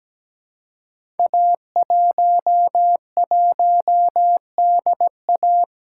Morse code, heard as a keyed tone, about 17 wpm, 700 Hz. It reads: A11DA
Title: Code Groups - 5 Letter:Numbers - 17wpm